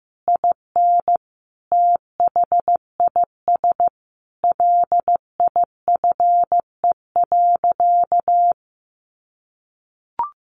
Morse code, heard as a keyed tone, about 15 wpm, 700 Hz. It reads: IN THIS LIFE.
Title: Sentences from Top 1200 Words 15wpm